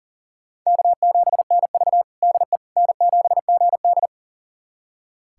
Morse code, heard as a keyed tone, about 40 wpm, 700 Hz. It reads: K7NV DE N7GD
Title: USA Callsign DE Callsign - 4 Characters - 40wpm